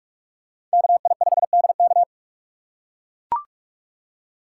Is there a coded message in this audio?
Yes: Morse code from 0.7 to 2.0 s, heard as a keyed tone, about 43 wpm, 700 Hz. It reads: KI5DK